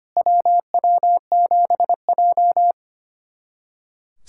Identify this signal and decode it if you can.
Morse code, heard as a keyed tone, about 25 wpm, 690 Hz. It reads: WW7J
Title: SOTA QSO Practice 25wpm